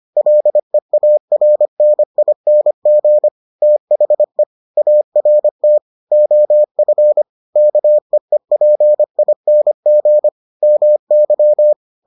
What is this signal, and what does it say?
Morse code, heard as a keyed tone, about 25 wpm, 600 Hz. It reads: LEARNING THE ART OF KEEPING MY